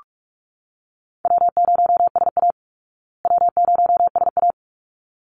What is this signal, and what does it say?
Morse code, heard as a keyed tone, about 45 wpm, 700 Hz. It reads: W0SA W0SA
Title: Easy Call Signs Repeated 2x - 45wpm